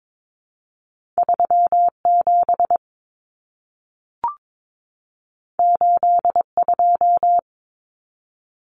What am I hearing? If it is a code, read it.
Morse code, heard as a keyed tone, about 22 wpm, 700 Hz. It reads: 37 82
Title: Numbers - 2-Digit 22wpm